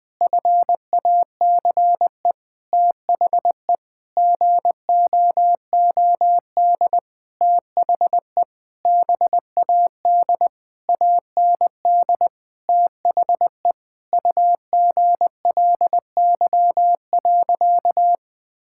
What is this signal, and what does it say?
Morse code, heard as a keyed tone, about 20 wpm, 700 Hz. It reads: FACE THE GOOD THE BAD AND THE UGLY.
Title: Sentences with 7-Word Phrases - 20wpm